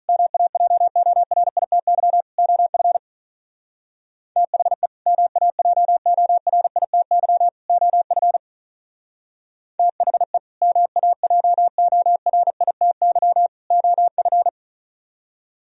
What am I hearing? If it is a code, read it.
Morse code, heard as a keyed tone, about 44 wpm, 700 Hz. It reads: MAJORITY OF THE MAJORITY OF THE MAJORITY OF